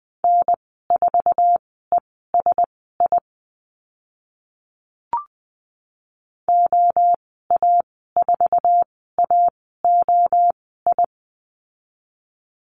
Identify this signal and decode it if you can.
Morse code, heard as a keyed tone, about 20 wpm, spaced as for 14 wpm, 700 Hz. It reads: N4ESI OA4AOI
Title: 069 Lesson 12g - Letter d - Call Signs 20wpm